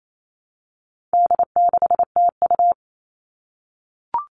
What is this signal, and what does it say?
Morse code, heard as a keyed tone, about 28 wpm, 700 Hz. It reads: D6TU
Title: All Cut Numbers - 4-Digit 28wpm